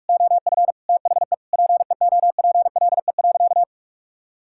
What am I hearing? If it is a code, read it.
Morse code, heard as a keyed tone, about 41 wpm, 700 Hz. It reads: OF THE PEOPLE.
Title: Sentences from Top 1600 Words 45wpm